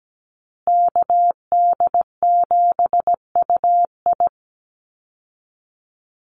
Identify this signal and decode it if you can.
Morse code, heard as a keyed tone, about 17 wpm, 700 Hz. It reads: KD7UI
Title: Call Signs Repeated 2x - 17wpm